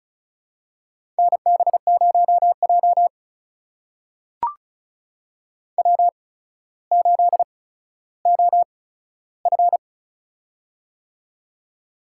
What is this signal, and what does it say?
Morse code, heard as a keyed tone, about 36 wpm, spaced as for 15 wpm, 700 Hz. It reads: NB0J W8OF
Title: USA Call Signs - 4 Characters - 8x Character Spacing - 35wpm